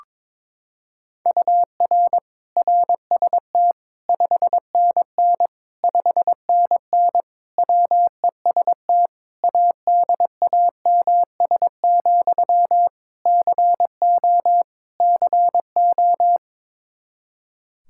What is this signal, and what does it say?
Morse code, heard as a keyed tone, about 22 wpm, 700 Hz. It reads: UR RST 5NN 5NN WEST ADAMS, CO CO
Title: K1USN SST QSO Practice 22wpm